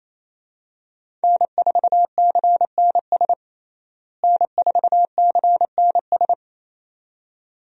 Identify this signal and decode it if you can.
Morse code, heard as a keyed tone, about 28 wpm, 700 Hz. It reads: N4CNS N4CNS
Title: USA Call Signs - 5 Characters - Repeated 2x - 28wpm